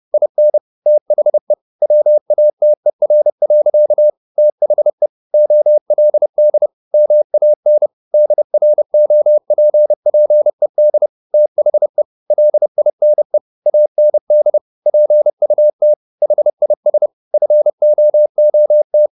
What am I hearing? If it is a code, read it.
Morse code, heard as a keyed tone, about 30 wpm, 600 Hz. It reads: IN THE WATER. THE OLD MAN DROPPED THE LINE AND PUT HIS FOOT